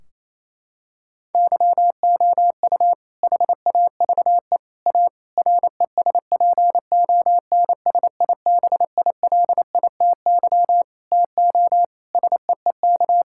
Morse code, heard as a keyed tone, about 28 wpm, 700 Hz. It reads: YOU HAVE A RESPONSIBILITY TO SEEK
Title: Sentences from Top 1600 Words 28wpm